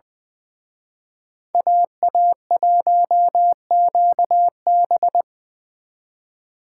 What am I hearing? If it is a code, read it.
Morse code, heard as a keyed tone, about 24 wpm, 700 Hz. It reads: AA1QB
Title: Speed Racing - USA 2x2 Call Signs 20wpm